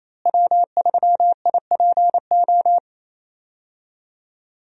Morse code, heard as a keyed tone, about 28 wpm, 700 Hz. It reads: W3IPO
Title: USA Call Signs - 5 Characters - Repeated 3x - 28wpm